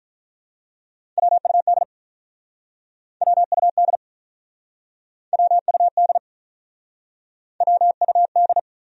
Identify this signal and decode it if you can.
Morse code, heard as a keyed tone, about 47 wpm, 700 Hz. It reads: WUD WUD WUD WUD